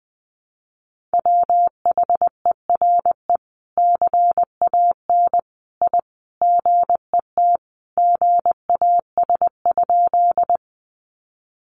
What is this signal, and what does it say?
Morse code, heard as a keyed tone, about 20 wpm, 700 Hz. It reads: WHERE CAN I GET GAS?
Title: Sentences from Top 1100 Words 20wpm